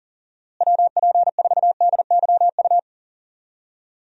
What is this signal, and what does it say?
Morse code, heard as a keyed tone, about 40 wpm, 700 Hz. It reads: WP4DYU
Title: USA Call Signs - 6 Characters - Repeated 2x - 40wpm